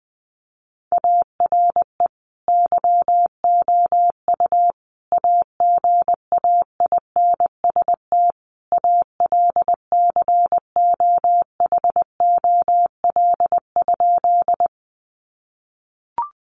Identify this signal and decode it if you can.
Morse code, heard as a keyed tone, about 19 wpm, 700 Hz. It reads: ARE YOU AGAINST ALCOHOL?
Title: Sentences from Top 1400 Words 20wpm